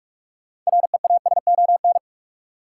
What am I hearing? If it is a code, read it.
Morse code, heard as a keyed tone, about 41 wpm, 700 Hz. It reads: REASON
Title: Sentences from Top 700 Words - New Word Drill - 45wpm